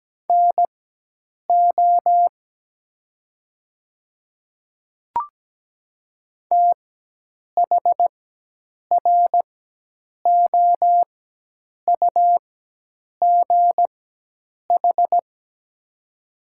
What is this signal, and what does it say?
Morse code, heard as a keyed tone, about 17 wpm, spaced as for 8 wpm, 700 Hz. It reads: NO THROUGH
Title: 4x Character Spacing - Top 300 Words - 17wpm